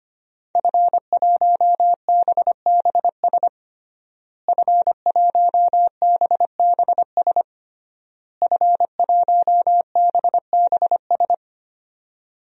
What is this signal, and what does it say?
Morse code, heard as a keyed tone, about 25 wpm, 700 Hz. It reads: F1BBS F1BBS F1BBS
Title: DXCC-100 Call Signs - 5 Characters - Repeated 3x - 25wpm